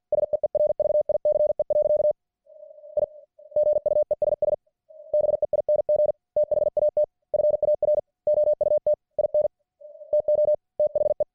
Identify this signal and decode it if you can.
Morse code, heard as a keyed tone, about 53 wpm, 600 Hz. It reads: SEERVICE. I GUESS BEING THAT FAR OUT IN TO THE